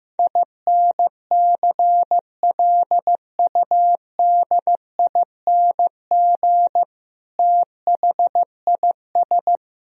Morse code, heard as a keyed tone, about 16 wpm, 700 Hz. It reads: INCLUDING THIS
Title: Sentences from Top 1000 Words - Review Part 4 - 15wpm